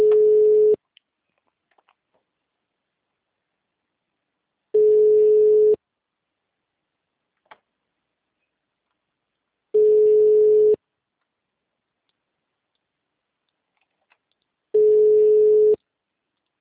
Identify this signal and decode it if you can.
Ringback tone (Europe, 425 Hz).
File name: out- -2002- - - .1609